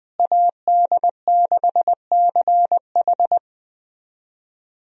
Morse code, heard as a keyed tone, about 20 wpm, 700 Hz. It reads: AD6CH
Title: USA Call Signs - 5 Characters - Repeated 2x - 20wpm